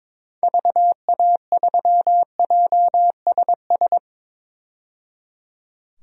Morse code, heard as a keyed tone, about 22 wpm, 700 Hz. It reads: VA3JSS
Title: DXCC-100 Call Signs - 6 Characters - 22wpm